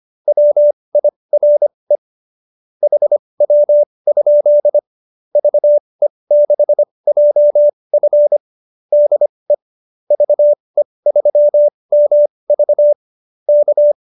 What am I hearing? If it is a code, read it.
Morse code, heard as a keyed tone, about 25 wpm, spaced as for 20 wpm, 590 Hz. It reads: WIRE HW? VE6JF DE VE3MV K
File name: QSO303 20